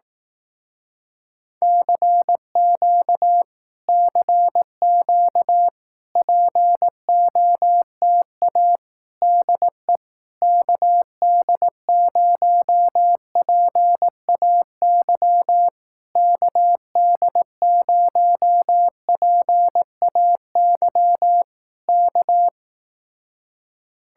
Morse code, heard as a keyed tone, about 18 wpm, 700 Hz. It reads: CQ CQ POTA DE KD0PAY KD0PAY K
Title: Speed-Racing Common QSO Phrases Encore - 15wpm